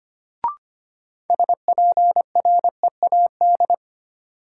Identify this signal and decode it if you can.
Morse code, heard as a keyed tone, about 25 wpm, 700 Hz. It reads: SPREAD
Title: Sentences from Top 1200 Words - New Word Drill - 25wpm